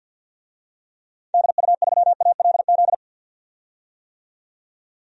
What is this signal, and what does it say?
Morse code, heard as a keyed tone, about 51 wpm, 700 Hz. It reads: DU3ALZ
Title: DXCC-100 Call Signs - 6 Characters - 50wpm